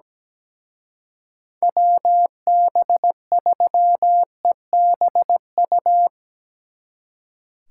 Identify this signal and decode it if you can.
Morse code, heard as a keyed tone, about 19 wpm, 700 Hz. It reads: WB3EBU
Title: Speed Racing - USA 2x3 Call Signs 17wpm